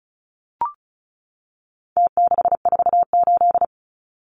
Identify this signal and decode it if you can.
Morse code, heard as a keyed tone, about 34 wpm, 700 Hz. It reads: T648
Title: Numbers - Contest Style - 4-Digit 35wpm